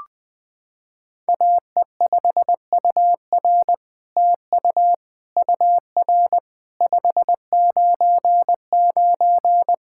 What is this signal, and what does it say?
Morse code, heard as a keyed tone, about 20 wpm, 700 Hz. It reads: AE5UR TU UR 599
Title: SOTA QSO Practice 20wpm